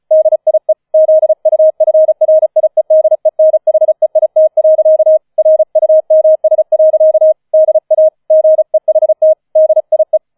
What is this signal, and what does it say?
Morse code, heard as a keyed tone, about 34 wpm, 630 Hz. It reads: DIE ZUFRIEDENHEIT. RUMS. DA GEHT DIE
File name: 004 Vierter Streich 35wpm